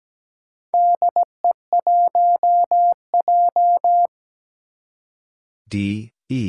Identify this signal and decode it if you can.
Morse code, heard as a keyed tone, about 17 wpm, 700 Hz. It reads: DE1J
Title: Four Challenging Letter-Number Combinations 17wpm